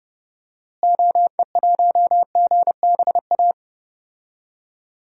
Morse code, heard as a keyed tone, about 32 wpm, 700 Hz. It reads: OE1GBA
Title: DXCC-100 Call Signs - 6 Characters - Repeated 3x - 30wpm